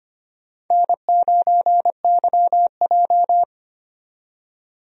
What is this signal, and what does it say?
Morse code, heard as a keyed tone, about 25 wpm, 700 Hz. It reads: N9YJ
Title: USA Call Signs - 4 Characters - 25wpm